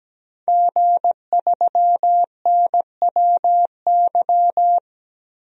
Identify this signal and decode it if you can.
Morse code, heard as a keyed tone, about 17 wpm, 700 Hz. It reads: G3NWY
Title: DXCC-100 Call Signs - 5 Characters - 17wpm